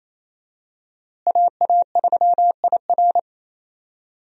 Morse code, heard as a keyed tone, about 28 wpm, 700 Hz. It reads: AA3IR
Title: USA Call Signs - 5 Characters - Repeated 3x - 28wpm